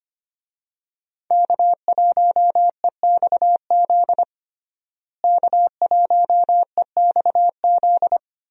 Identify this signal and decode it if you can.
Morse code, heard as a keyed tone, about 25 wpm, 700 Hz. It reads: K1EXZ K1EXZ
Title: USA Call Signs - 5 Characters - Repeated 3x - 25wpm